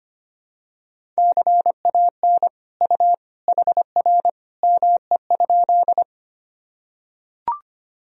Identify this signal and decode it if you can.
Morse code, heard as a keyed tone, about 25 wpm, 700 Hz. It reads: CAN U HR ME?
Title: Sentences from Top 300 Words and Abbreviations 25wpm